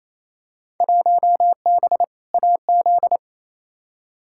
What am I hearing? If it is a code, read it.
Morse code, heard as a keyed tone, about 28 wpm, 700 Hz. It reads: 1B AZ
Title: Field Day Exchanges Encore - 28wpm